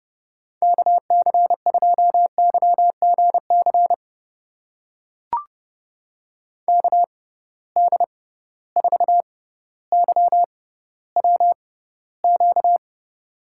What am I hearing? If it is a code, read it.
Morse code, heard as a keyed tone, about 31 wpm, spaced as for 15 wpm, 700 Hz. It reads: KC2YGC KD4YWQ
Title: USA Call Signs - 6 Characters - 6x Character Spacing - 30wpm